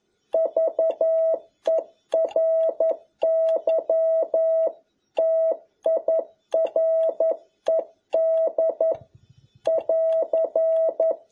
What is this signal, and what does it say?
Morse code, heard as a keyed tone, about 11 wpm, 650 Hz. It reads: VERY TIRED <AR>
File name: jh8cbh / 2021cw